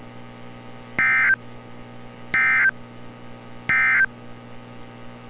Emergency Alert System (SAME) tones, heard as harsh buzzing bursts. It is heard from 0.9 to 4.1 s.
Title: NOAA Weather Radio - Altoona, PA